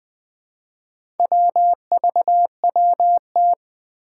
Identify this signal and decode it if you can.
Morse code, heard as a keyed tone, about 20 wpm, 700 Hz. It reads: WVWT
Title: Four Challenging Letter-Number Combinations 20wpm